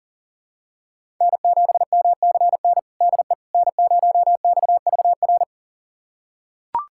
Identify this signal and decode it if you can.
Morse code, heard as a keyed tone, about 40 wpm, 700 Hz. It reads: N7MCN DE N0XVR
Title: USA Callsign DE Callsign - 5 Characters - 40wpm